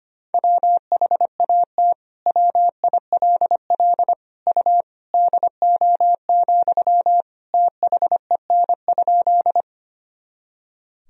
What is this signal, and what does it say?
Morse code, heard as a keyed tone, about 25 wpm, 700 Hz. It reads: WHAT WILL U DO, THEN?
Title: Sentences from Top 100 Words and Abbreviations 25wpm